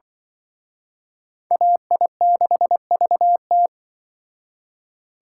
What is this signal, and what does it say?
Morse code, heard as a keyed tone, about 24 wpm, 700 Hz. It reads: AI6VT
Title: Speed Racing - USA 2x2 Call Signs 20wpm